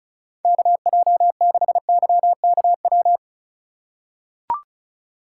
Morse code, heard as a keyed tone, about 36 wpm, 700 Hz. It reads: KJ6YKW
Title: USA Call Signs - 2x Character Spacing - 35wpm